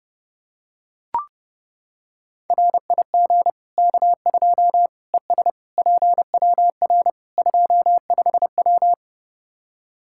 Morse code, heard as a keyed tone, about 30 wpm, 700 Hz. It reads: RIG K2 ES PWR 25W
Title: Practice QSO - Exchange 2 - Info - 30wpm